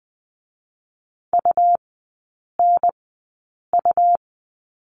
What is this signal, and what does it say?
Morse code, heard as a keyed tone, about 20 wpm, 700 Hz. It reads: U N U
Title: 100 Lesson 17e - Letter u - Three Characters 20wpm